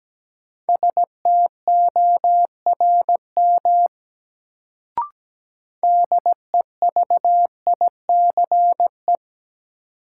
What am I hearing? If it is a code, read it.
Morse code, heard as a keyed tone, about 17 wpm, 700 Hz. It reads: STORM DEVICE
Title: Sentences from Top 1500 Words - New Word Drill Alt - 17wpm